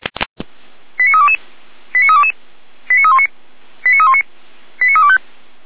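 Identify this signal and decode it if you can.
Five-tone selective call (ZVEI), heard as quick runs of five tones, tones of about 70 ms. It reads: 89322 89322 89319 89319 89325
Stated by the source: ZVEI five-tone selective call received on an SDR.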